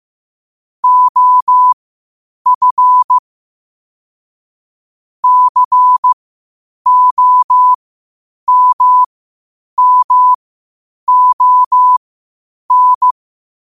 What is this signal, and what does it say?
Morse code, heard as a keyed tone, about 15 wpm, spaced as for 8 wpm, 1000 Hz. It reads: OF COMMON
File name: qotdfiles / 8wpm